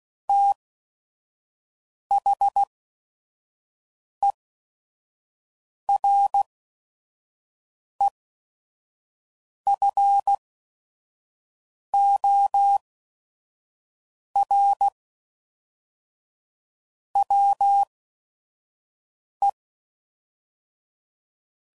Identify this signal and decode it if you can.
Morse code, heard as a keyed tone, about 16 wpm, spaced as for 5 wpm, 800 Hz. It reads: THEREFOR WE
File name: CW Code Groups @5WPM